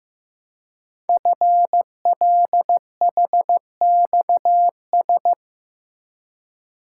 Morse code, heard as a keyed tone, about 15 wpm, 700 Hz. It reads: FLHXS
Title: Code Groups - 5 Letters - 15wpm